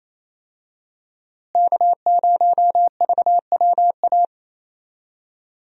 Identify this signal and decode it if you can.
Morse code, heard as a keyed tone, about 28 wpm, 700 Hz. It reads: K0VWA
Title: USA Call Signs - 5 Characters - Repeated 3x - 28wpm